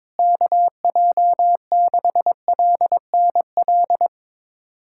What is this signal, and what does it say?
Morse code, heard as a keyed tone, about 22 wpm, 700 Hz. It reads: KJ6LNL
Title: USA Callsign DE Callsign - 6 Characters - 22wpm